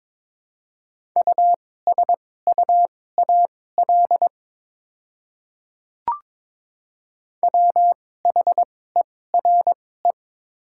Morse code, heard as a keyed tone, about 22 wpm, spaced as for 16 wpm, 700 Hz. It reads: USUAL WHERE
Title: 2x Character Spacing - Top 500 Words - 22wpm